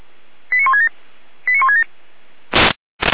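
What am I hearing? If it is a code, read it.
Five-tone selective call (ZVEI), heard as quick runs of five tones, tones of about 70 ms. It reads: 89167 89167
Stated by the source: ZVEI five-tone selective call received on an SDR.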